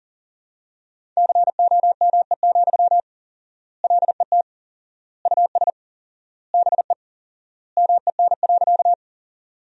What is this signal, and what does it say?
Morse code, heard as a keyed tone, about 40 wpm, 700 Hz. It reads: COME, LET US BE MEN.
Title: 4x Word-Spacing - Sentences from Top 300 Words - 40wpm